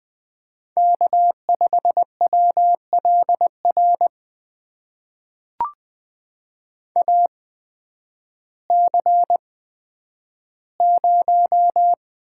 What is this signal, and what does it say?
Morse code, heard as a keyed tone, about 20 wpm, spaced as for 11 wpm, 700 Hz. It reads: K5WLR AC0
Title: USA Call Signs - 5 Characters - 8x Character Spacing - 20wpm